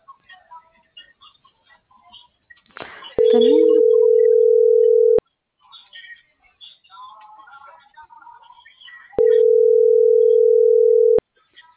Ringback tone (North America), 440 + 480 Hz.